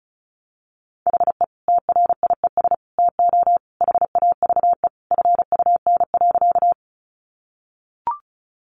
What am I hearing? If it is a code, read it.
Morse code, heard as a keyed tone, about 34 wpm, 700 Hz. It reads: HE TRIES TO HAVE FUN.
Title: Easy Phrases 35wpm